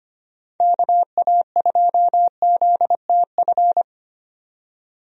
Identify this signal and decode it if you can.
Morse code, heard as a keyed tone, about 25 wpm, 700 Hz. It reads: KA2ZTF
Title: USA Call Signs - 6 Characters - 25wpm